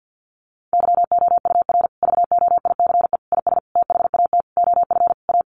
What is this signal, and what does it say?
Morse code, heard as a keyed tone, about 50 wpm, 700 Hz. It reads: YOUR VOICE IS THAT OF A
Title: Sentences from Top 500 Words 50wpm